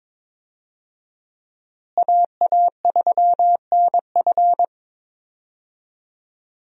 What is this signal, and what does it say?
Morse code, heard as a keyed tone, about 22 wpm, 700 Hz. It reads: AA3NF
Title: USA Call Signs - 5 Characters - Repeated 3x - 22wpm